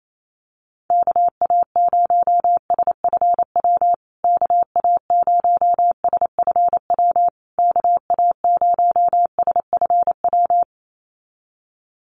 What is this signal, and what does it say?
Morse code, heard as a keyed tone, about 28 wpm, 700 Hz. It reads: KA0SFW KA0SFW KA0SFW
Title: USA Call Signs Repeated 3x 28wpm